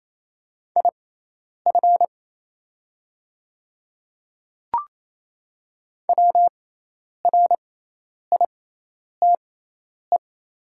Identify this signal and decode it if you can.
Morse code, heard as a keyed tone, about 28 wpm, spaced as for 10 wpm, 700 Hz. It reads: IF WRITE
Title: 6x Character Spacing - Top 200 Words - 28wpm